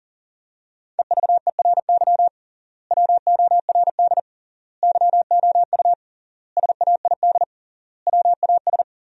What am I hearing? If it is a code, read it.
Morse code, heard as a keyed tone, about 40 wpm, 700 Hz. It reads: EVERY WORD YOU SAID WAS
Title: 3x Word-Spacing - Sentences from Top 400 Words - 40wpm